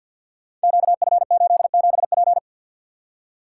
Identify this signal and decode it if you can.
Morse code, heard as a keyed tone, about 48 wpm, 700 Hz. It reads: QF87P
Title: Code Groups - 5 Letter:Numbers - 50wpm